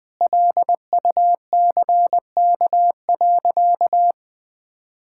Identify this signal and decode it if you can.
Morse code, heard as a keyed tone, about 20 wpm, 700 Hz. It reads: LUCK.
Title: Sentences with 5-Word Phrases - 20wpm